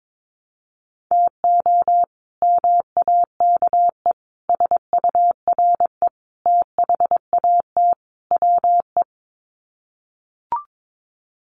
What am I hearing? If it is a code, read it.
Morse code, heard as a keyed tone, about 22 wpm, 700 Hz. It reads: TO MAKE SURE THAT WE
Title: Sets of 5 Words Encore 22wpm